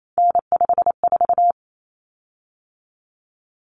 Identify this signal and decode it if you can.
Morse code, heard as a keyed tone, about 28 wpm, 700 Hz. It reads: N54
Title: Common Cut Numbers - 3-Digit 28wpm